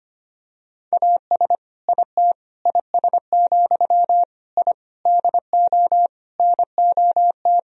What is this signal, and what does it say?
Morse code, heard as a keyed tone, about 25 wpm, 700 Hz. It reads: AS IT IS, I DO NOT
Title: Sentences from Top 1000 Words - Review Part 2 - 25wpm